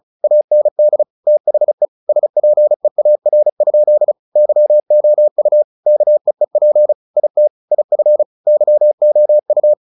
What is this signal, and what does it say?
Morse code, heard as a keyed tone, about 34 wpm, 600 Hz. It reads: AND THE SPEAR? YOU KEEP IT IF YOU
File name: the old man and the sea 2 35wpm